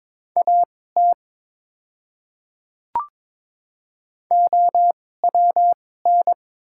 Morse code, heard as a keyed tone, about 22 wpm, spaced as for 16 wpm, 700 Hz. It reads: AT OWN
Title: 2x Character Spacing - Top 300 Words - 22wpm